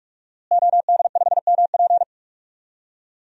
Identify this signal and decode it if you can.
Morse code, heard as a keyed tone, about 42 wpm, 700 Hz. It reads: OD5MP
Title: DXCC-100 Call Signs - 5 Characters - Repeated 2x - 45wpm